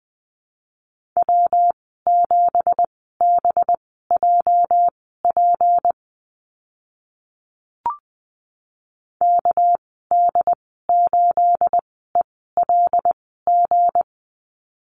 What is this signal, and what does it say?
Morse code, heard as a keyed tone, about 20 wpm, spaced as for 15 wpm, 700 Hz. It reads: W7BJP KD8ELG
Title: 213 Lesson 34g - Number 8 - Call Signs 20wpm